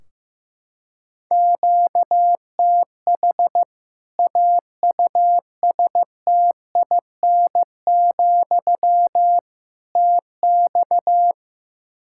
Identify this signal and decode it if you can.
Morse code, heard as a keyed tone, about 15 wpm, 700 Hz. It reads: QTH AUSTIN, TX
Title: Practice QSO - Exchange 1 - Beginning - 15wpm